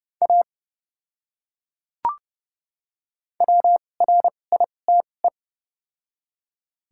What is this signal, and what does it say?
Morse code, heard as a keyed tone, about 31 wpm, spaced as for 22 wpm, 700 Hz. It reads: A WRITE
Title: 2x Character Spacing - Top 300 Words - 30wpm